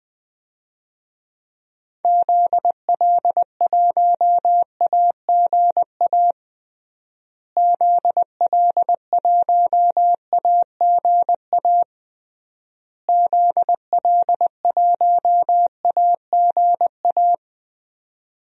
Morse code, heard as a keyed tone, about 21 wpm, 700 Hz. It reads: ZL1AGA ZL1AGA ZL1AGA
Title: DXCC-100 Call Signs - 6 Characters - Repeated 3x - 20wpm